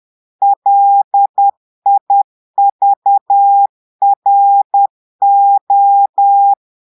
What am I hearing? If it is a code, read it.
Morse code, heard as a keyed tone, about 10 wpm, 800 Hz. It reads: LIVRO
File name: Portugais - texte en clair 1 - vitesse - 10 mots minute